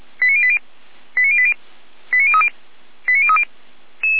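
Five-tone selective call (ZVEI), heard as quick runs of five tones, tones of about 70 ms. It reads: 89080 89080 89030 89030
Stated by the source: Five-tone ZVEI selective call with code 89084, received on an SDR.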